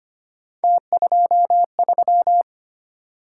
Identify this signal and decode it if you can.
Morse code, heard as a keyed tone, about 25 wpm, 700 Hz. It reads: T23
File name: Three Letter-Number Combinations - 25wpm